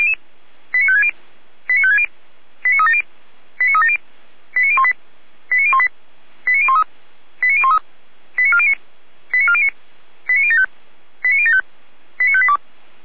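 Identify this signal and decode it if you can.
Five-tone selective call (ZVEI), heard as quick runs of five tones, tones of about 70 ms. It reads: 89560 89560 89370 89370 89018 89018 89013 89013 89409 89409 89075 89075 89582
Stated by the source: ZVEI five-tone selective call received by SDR, recorded on the audio server of Feuerwehr Orlamünde.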